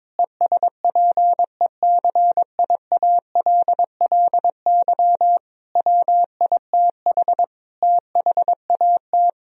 Morse code, heard as a keyed tone, about 22 wpm, 700 Hz. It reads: ESPECIALLY WITH THAT
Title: Sentences from Top 1000 Words - Review Part 4 - 22wpm